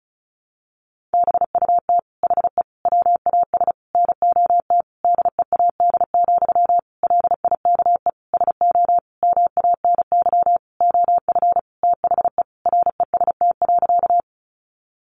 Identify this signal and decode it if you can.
Morse code, heard as a keyed tone, about 34 wpm, 700 Hz. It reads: BUT HE WAS NOT DEAD, LIKE SO MANY OF THE REST.
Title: Sentences from Top 1500 Words 35wpm